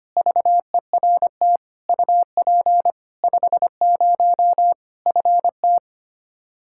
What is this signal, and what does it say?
Morse code, heard as a keyed tone, about 25 wpm, 700 Hz. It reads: VERT UP 50 FT
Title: Practice QSO - Exchange 2 - Info - 25wpm